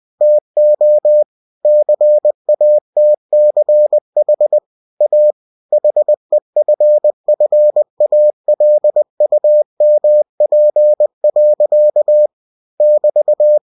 Morse code, heard as a keyed tone, about 20 wpm, 600 Hz. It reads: TO CATCH A HEFFALUMP. <BT>
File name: cw-books / winnie-the-pooh / winnie-20wpm-page-0034